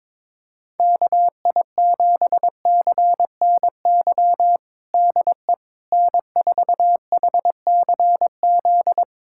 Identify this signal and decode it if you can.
Morse code, heard as a keyed tone, about 22 wpm, 700 Hz. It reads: KI7CNY DE N4HCZ
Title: USA Callsign DE Callsign 22wpm